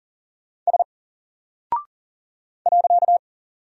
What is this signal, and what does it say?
Morse code, heard as a keyed tone, about 40 wpm, 700 Hz. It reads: S .
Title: Single Character (Letter:Number:Punctuation:Prosign) Alt 40wpm